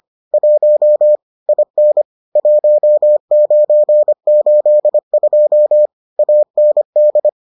Morse code, heard as a keyed tone, about 25 wpm, 600 Hz. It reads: 1 IN 1982 AND